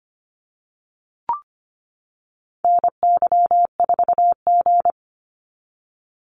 Morse code, heard as a keyed tone, about 25 wpm, 690 Hz. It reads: NY4G
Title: NA SOTA Activator Callsigns 25wpm